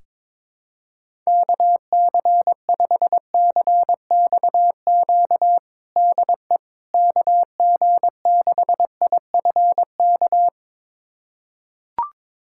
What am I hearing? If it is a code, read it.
Morse code, heard as a keyed tone, about 22 wpm, 700 Hz. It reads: KC5CXQ DE KG6IFK
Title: USA Callsign DE Callsign 22wpm